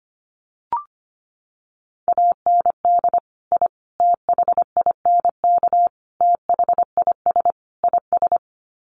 Morse code, heard as a keyed tone, about 25 wpm, 700 Hz. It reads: AND I THINK THIS IS
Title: Sets of 5 Words Encore 25wpm